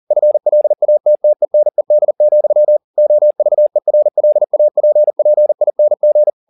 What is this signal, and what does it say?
Morse code, heard as a keyed tone, about 40 wpm, 600 Hz. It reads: FLATTENED, OVERLAPPING